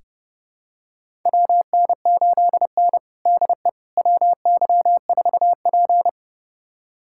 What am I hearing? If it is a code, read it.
Morse code, heard as a keyed tone, about 29 wpm, 700 Hz. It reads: WN8N DE WY4P
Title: USA Callsign DE Callsign - 4 Characters - 30wpm